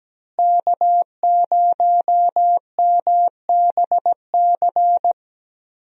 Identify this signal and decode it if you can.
Morse code, heard as a keyed tone, about 17 wpm, 700 Hz. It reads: K0MBC
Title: USA Call Signs - 5 Characters - Repeated 3x - 17wpm